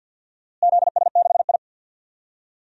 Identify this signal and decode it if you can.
Morse code, heard as a keyed tone, about 47 wpm, 700 Hz. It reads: ZS6I
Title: DXCC-100 Call Signs - 4 Characters - Repeated 2x - 50wpm